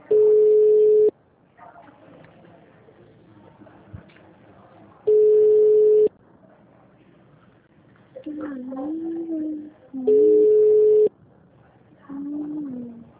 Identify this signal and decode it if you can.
Ringback tone (Europe, 425 Hz), with speech.